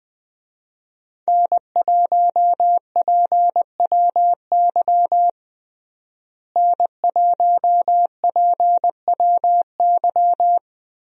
Morse code, heard as a keyed tone, about 21 wpm, 700 Hz. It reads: N1PWY N1PWY
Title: USA Call Signs - 5 Characters - Repeated 2x - 20wpm